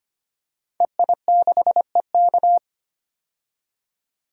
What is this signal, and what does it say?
Morse code, heard as a keyed tone, about 25 wpm, 700 Hz. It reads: EI6EK
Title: DXCC-100 Call Signs - 5 Characters - 25wpm